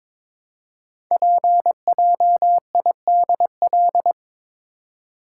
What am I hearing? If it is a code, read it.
Morse code, heard as a keyed tone, about 22 wpm, 700 Hz. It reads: PJIDL
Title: Code Groups - 5 Letters - 22wpm